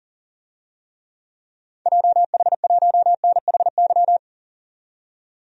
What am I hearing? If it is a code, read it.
Morse code, heard as a keyed tone, about 40 wpm, 700 Hz. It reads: JH1NHY
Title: DXCC-100 Call Signs - 6 Characters - 40wpm